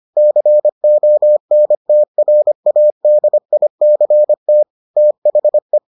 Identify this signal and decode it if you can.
Morse code, heard as a keyed tone, about 25 wpm, 600 Hz. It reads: CONTRADICT THE